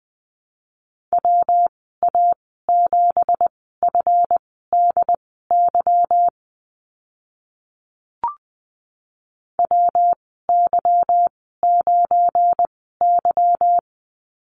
Morse code, heard as a keyed tone, about 20 wpm, spaced as for 15 wpm, 700 Hz. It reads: WA7FDY WY9Y
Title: 186 Lesson 30g - Letter b - Call Signs 20wpm